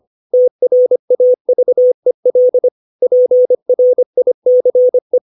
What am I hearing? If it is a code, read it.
Morse code, heard as a keyed tone, about 25 wpm, 500 Hz. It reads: TRAVEL PRICE
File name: 632 words / 632 Words 25WPM